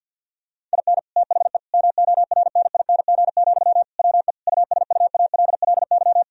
Morse code, heard as a keyed tone, about 50 wpm, 700 Hz. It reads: IN THE MORNING, WE USUALLY
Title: Sentences from Top 1000 Words - Review Part 3 - 50wpm